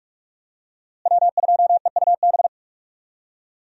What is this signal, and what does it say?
Morse code, heard as a keyed tone, about 41 wpm, 700 Hz. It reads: W2EUB
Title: USA Call Signs - 5 Characters - Repeated 2x - 45wpm